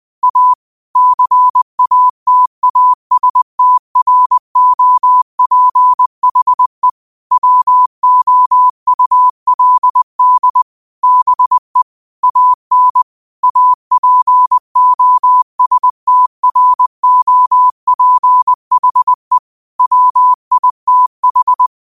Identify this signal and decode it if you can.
Morse code, heard as a keyed tone, about 20 wpm, 1000 Hz. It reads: A CATASTROPHE WOULD BE AN APOSTROPHE WITH
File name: qotdfiles / 20wpm